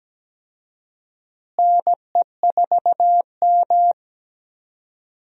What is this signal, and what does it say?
Morse code, heard as a keyed tone, about 17 wpm, 700 Hz. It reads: NE4M
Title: USA Call Signs - 4 Characters - Repeated 3x - 17wpm